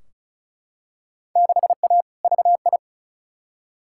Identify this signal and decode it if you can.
Morse code, heard as a keyed tone, about 36 wpm, 700 Hz. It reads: 6A VI
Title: Field Day Exchanges Encore - 35wpm